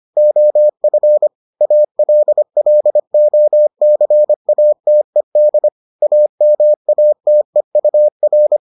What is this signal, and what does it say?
Morse code, heard as a keyed tone, about 25 wpm, 600 Hz. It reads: OF ALLOCATED AMATEUR